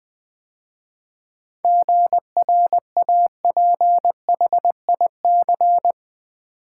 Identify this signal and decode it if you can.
Morse code, heard as a keyed tone, about 20 wpm, 700 Hz. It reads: GRAPHIC
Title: 7 Letter Words - First 3 Letters Given - 20wpm